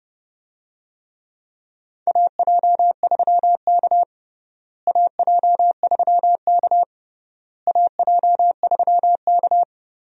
Morse code, heard as a keyed tone, about 30 wpm, 700 Hz. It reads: AJ3K AJ3K AJ3K
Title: USA Call Signs - 4 Characters - Repeated 3x - 30wpm